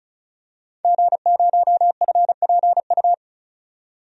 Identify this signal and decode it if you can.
Morse code, heard as a keyed tone, about 34 wpm, 700 Hz. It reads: G0FPU
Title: DXCC-100 Call Signs - 5 Characters - 35wpm